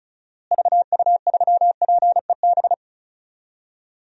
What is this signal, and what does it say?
Morse code, heard as a keyed tone, about 36 wpm, 700 Hz. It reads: VU3PEB
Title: DXCC-100 Call Signs - 6 Characters - Repeated 3x - 35wpm